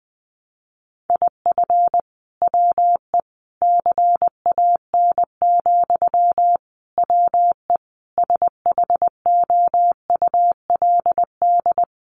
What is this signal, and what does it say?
Morse code, heard as a keyed tone, about 20 wpm, 700 Hz. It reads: IF WE CAN, WE SHOULD
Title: Sentences from Top 1100 Words 20wpm